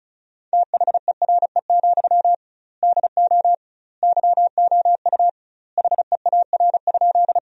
Morse code, heard as a keyed tone, about 34 wpm, 700 Hz. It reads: THERE, DO YOU HEAR?
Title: 2x Word-Spacing - Sentences from Top 300 Words - 35wpm